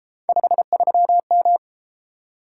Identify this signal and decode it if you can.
Morse code, heard as a keyed tone, about 32 wpm, 700 Hz. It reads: 53M